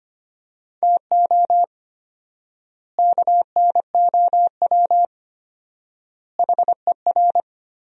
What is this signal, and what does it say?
Morse code, heard as a keyed tone, about 25 wpm, 700 Hz. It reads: TO KNOW HER
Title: 4x Word-Spacing - Sentences from Top 400 Words - 25wpm